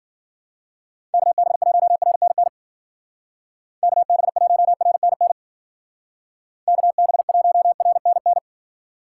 Morse code, heard as a keyed tone, about 53 wpm, 700 Hz. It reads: KB1RNN KB1RNN KB1RNN